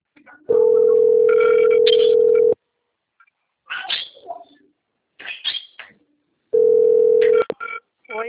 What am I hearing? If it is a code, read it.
Ringback tone (North America), 440 + 480 Hz.